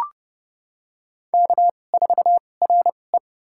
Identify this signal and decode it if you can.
Morse code, heard as a keyed tone, about 30 wpm, spaced as for 22 wpm, 690 Hz. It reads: K4RE
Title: USA Call Signs - 4 Characters - 2x Character Spacing - 30wpm